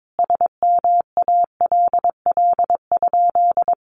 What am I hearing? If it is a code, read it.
Morse code, heard as a keyed tone, about 22 wpm, 700 Hz. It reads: SMALL?
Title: Sentences from Top 1200 Words 22wpm